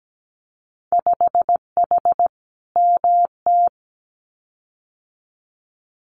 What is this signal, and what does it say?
Morse code, heard as a keyed tone, about 17 wpm, 700 Hz. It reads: 5H MT
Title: Winter Field Day Exchanges Encore 17wpm